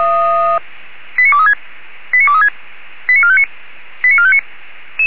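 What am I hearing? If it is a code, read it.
Five-tone selective call (ZVEI), heard as quick runs of five tones, tones of about 70 ms. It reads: 89236 89236 89459 89459
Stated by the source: ZVEI five-tone call received via SDR.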